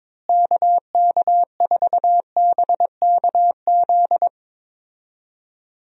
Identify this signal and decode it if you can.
Morse code, heard as a keyed tone, about 22 wpm, 700 Hz. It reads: KK4BKZ
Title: USA Call Signs - 6 Characters - Repeated 3x - 22wpm